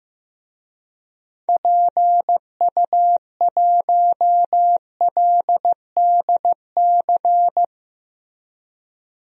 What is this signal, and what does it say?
Morse code, heard as a keyed tone, about 15 wpm, 700 Hz. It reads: PU1LDC
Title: DXCC-100 Call Signs - 6 Characters - Repeated 2x - 15wpm